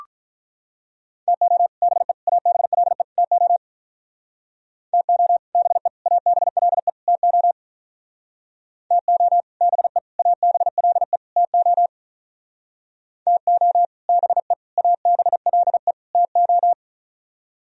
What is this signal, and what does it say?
Morse code, heard as a keyed tone, about 43 wpm, 700 Hz. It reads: TO BE ABLE TO TO BE ABLE TO TO BE ABLE TO TO BE ABLE TO
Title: Speed-Racing - Sets of 4 Words 35wpm